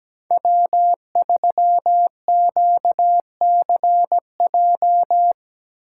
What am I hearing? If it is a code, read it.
Morse code, heard as a keyed tone, about 17 wpm, 700 Hz. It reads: W3QCJ
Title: USA Call Signs - 5 Characters - 4x Character Spacing - 17wpm